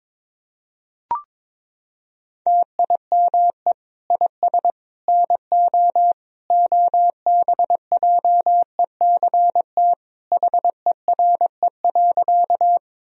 Morse code, heard as a keyed tone, about 22 wpm, 700 Hz. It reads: TIME IS NO OBJECT HERE.
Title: Sentences from Top 1000 Words - Review Part 3 - 22wpm